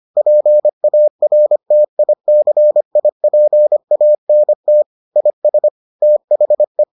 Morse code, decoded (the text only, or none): PARTICIPANT IS THE